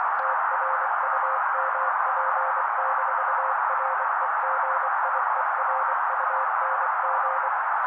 Morse code, heard as a keyed tone, about 24 wpm, 530 Hz. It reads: TRUMP-REGIERUNG
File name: daily / 23wpm noise